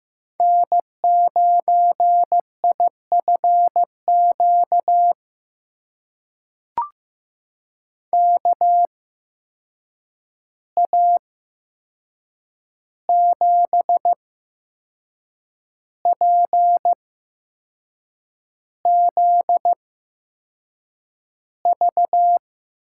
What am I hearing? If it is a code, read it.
Morse code, heard as a keyed tone, about 15 wpm, spaced as for 6 wpm, 700 Hz. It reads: N9IFQ KA7PZV
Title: USA Call Signs - 8x Character Spacing - 15wpm